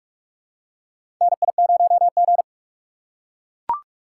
Morse code, heard as a keyed tone, about 43 wpm, 700 Hz. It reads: NI0G
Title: USA Call Signs - 4 Characters - 6x Character Spacing - 45wpm